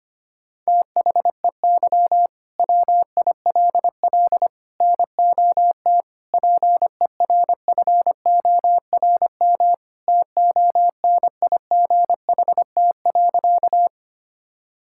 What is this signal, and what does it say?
Morse code, heard as a keyed tone, about 25 wpm, 700 Hz. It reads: THEY WILL NOT PERFORM TONIGHT.
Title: Sentences from Top 1000 Words - Review Part 5 - 25wpm